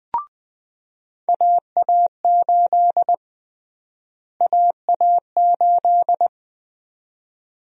Morse code, heard as a keyed tone, about 20 wpm, 700 Hz. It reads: AA8 AA8
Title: USA Call Sign Prefixes - Repeated 2x - 20wpm